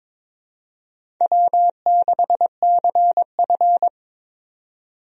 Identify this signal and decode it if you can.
Morse code, heard as a keyed tone, about 22 wpm, 700 Hz. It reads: W6CF
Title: USA Call Signs - 4 Characters - 22wpm